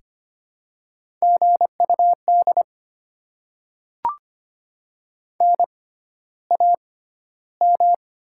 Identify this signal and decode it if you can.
Morse code, heard as a keyed tone, about 25 wpm, spaced as for 13 wpm, 700 Hz. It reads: GUD NAM
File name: Top 20 QSO Elements - 6x Character Spacing - 25wpm